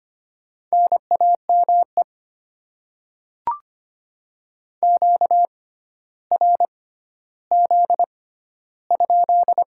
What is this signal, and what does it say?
Morse code, heard as a keyed tone, about 25 wpm, spaced as for 13 wpm, 690 Hz. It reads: NAME QRZ?
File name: Top 90 QSO Elements - 6x Character Spacing - 25wpm